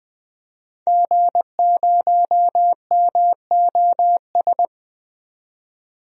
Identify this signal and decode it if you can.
Morse code, heard as a keyed tone, about 20 wpm, 700 Hz. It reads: G0MOS
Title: DXCC-100 Call Signs - 5 Characters - 20wpm